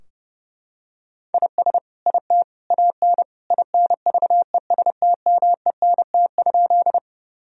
Morse code, heard as a keyed tone, about 32 wpm, 700 Hz. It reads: IS IT AN INVESTMENT?
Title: Sentences from Top 1000 Words - Review Part 4 - 30wpm